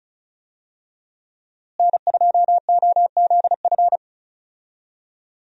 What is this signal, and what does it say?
Morse code, heard as a keyed tone, about 36 wpm, 700 Hz. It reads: N2OZF
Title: USA Call Signs - 5 Characters - Repeated 2x - 35wpm